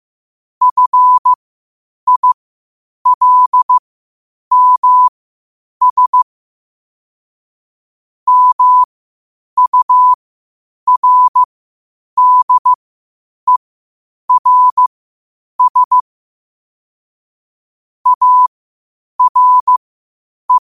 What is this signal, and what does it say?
Morse code, heard as a keyed tone, about 15 wpm, spaced as for 8 wpm, 1000 Hz. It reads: FILMS MURDERS ARE
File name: qotdfiles / 8wpm